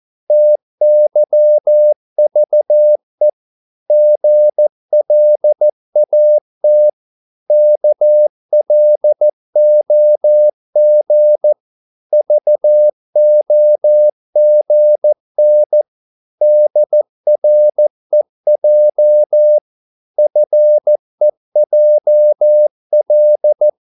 Morse code, heard as a keyed tone, about 14 wpm, 600 Hz. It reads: TYVE GLAT KLOG VOGN DREJ FEJL
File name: Korte ord DK 14wpm